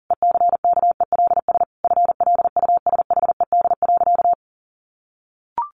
Morse code, heard as a keyed tone, about 40 wpm, 700 Hz. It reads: ECKELS FLUSHED.
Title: A Sound Of Thunder 40wpm